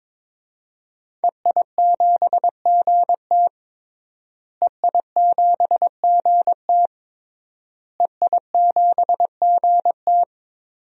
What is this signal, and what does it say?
Morse code, heard as a keyed tone, about 22 wpm, 700 Hz. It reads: EI7GT EI7GT EI7GT
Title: DXCC-100 Call Signs - 5 Characters - Repeated 3x - 22wpm